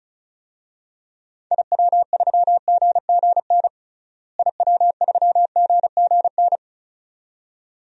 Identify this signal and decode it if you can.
Morse code, heard as a keyed tone, about 34 wpm, 700 Hz. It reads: IW3GGN IW3GGN
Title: DXCC-100 Call Signs - 6 Characters - Repeated 2x - 35wpm